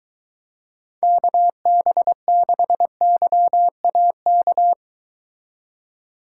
Morse code, heard as a keyed tone, about 22 wpm, 700 Hz. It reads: KB6YAK